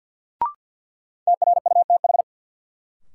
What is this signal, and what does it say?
Morse code, heard as a keyed tone, about 53 wpm, 700 Hz. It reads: TRUTH
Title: Sets of 5 Words - New Word Drill - 50wpm